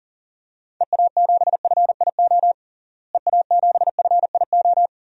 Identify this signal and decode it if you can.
Morse code, heard as a keyed tone, about 37 wpm, 700 Hz. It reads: EA7FIO EA7FIO
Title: DXCC-100 Call Signs - 6 Characters - Repeated 2x - 40wpm